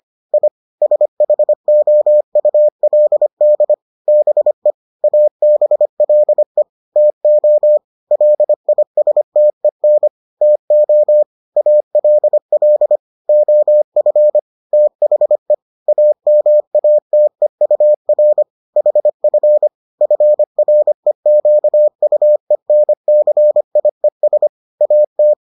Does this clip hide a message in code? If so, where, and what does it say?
Yes: Morse code from 0.3 to 25.3 s, heard as a keyed tone, about 25 wpm, 600 Hz. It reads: I SHOULD BE ABLE TO LISTEN TO ALL OF THE AMATEUR HF FREQUENCIES AT